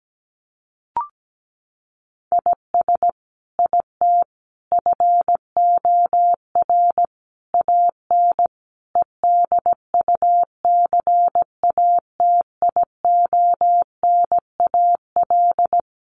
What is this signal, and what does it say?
Morse code, heard as a keyed tone, about 17 wpm, 700 Hz. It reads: IS IT FOR AN EDUCATIONAL
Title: Sentences from Top 1200 Words 17wpm